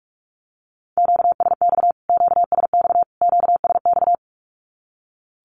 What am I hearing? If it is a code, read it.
Morse code, heard as a keyed tone, about 44 wpm, 700 Hz. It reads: QSX QSX QSX
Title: Q-Codes Repeated 3x 45wpm